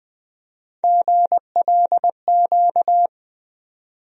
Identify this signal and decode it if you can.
Morse code, heard as a keyed tone, about 20 wpm, 700 Hz. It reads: GLQ
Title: USA Call Sign Postfixes - 3 Letters 20wpm